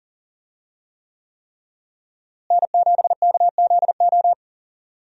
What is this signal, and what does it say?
Morse code, heard as a keyed tone, about 40 wpm, 700 Hz. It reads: N7KZO
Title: USA Call Signs - 5 Characters - Repeated 3x - 40wpm